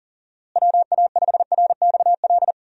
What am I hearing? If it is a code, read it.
Morse code, heard as a keyed tone, about 40 wpm, 700 Hz. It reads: WA5RXL
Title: USA Call Signs - 6 Characters - Repeated 3x - 40wpm